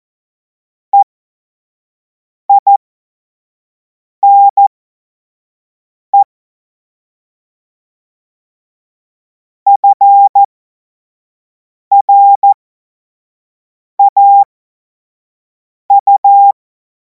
Morse code, heard as a keyed tone, about 14 wpm, spaced as for 5 wpm, 800 Hz. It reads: EINE FRAU
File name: Morsekurs / Witze / Witz 03. März 2026 25 F70